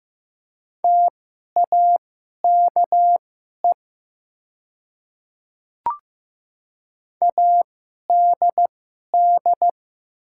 Morse code, heard as a keyed tone, about 15 wpm, spaced as for 11 wpm, 690 Hz. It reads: TAKE ADD
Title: 2x Character Spacing - Top 200 Words - 15wpm